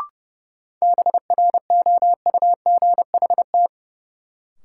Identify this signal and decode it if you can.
Morse code, heard as a keyed tone, about 30 wpm, 700 Hz. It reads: BROUGHT
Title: Sentences from Top 500 Words - New Word Drill - 30wpm